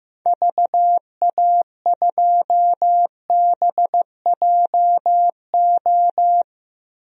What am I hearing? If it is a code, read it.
Morse code, heard as a keyed tone, about 15 wpm, 700 Hz. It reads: VA2BJO
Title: DXCC-100 Call Signs - 6 Characters - 15wpm